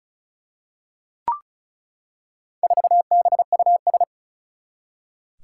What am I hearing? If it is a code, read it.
Morse code, heard as a keyed tone, about 36 wpm, 700 Hz. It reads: 4BUS
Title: Four Challenging Letter-Number Combinations 35wpm